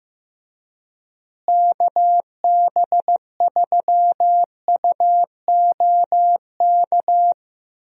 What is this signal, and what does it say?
Morse code, heard as a keyed tone, about 16 wpm, 700 Hz. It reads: KB3UOK
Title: USA Call Signs - 6 Characters - Repeated 3x - 15wpm